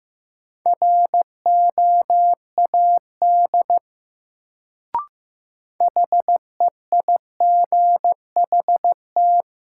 Morse code, heard as a keyed tone, about 15 wpm, 700 Hz. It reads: ROAD HEIGHT
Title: Sets of 5 Words - New Word Drill - 15wpm